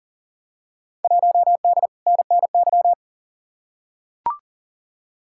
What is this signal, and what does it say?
Morse code, heard as a keyed tone, about 38 wpm, 700 Hz. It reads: 1D NNY
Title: Field Day Exchanges - 40wpm